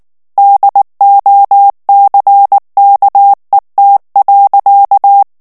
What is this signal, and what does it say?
Morse code, heard as a keyed tone, about 19 wpm, 800 Hz. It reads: DOCKET.
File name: morse mp3 / ANGLAIS texte clair 3 vitesse 18 mots minute